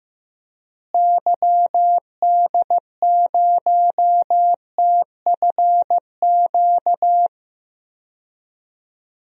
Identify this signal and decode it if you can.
Morse code, heard as a keyed tone, about 15 wpm, 700 Hz. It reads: YD0TFQ
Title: DXCC-100 Call Signs - 6 Characters - Repeated 3x - 15wpm